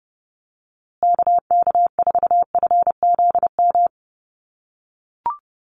Morse code, heard as a keyed tone, about 28 wpm, 700 Hz. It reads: KK4FZM
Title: USA Call Signs Repeated 2x 30wpm